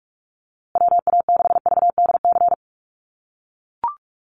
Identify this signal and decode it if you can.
Morse code, heard as a keyed tone, about 44 wpm, 690 Hz. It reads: WA6VDC
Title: USA Call Signs Repeated 2x 45wpm